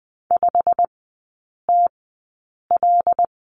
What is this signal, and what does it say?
Morse code, heard as a keyed tone, about 20 wpm, spaced as for 8 wpm, 700 Hz. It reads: 5TL
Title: 191 Lesson 31e - Letter v - Three Characters 20wpm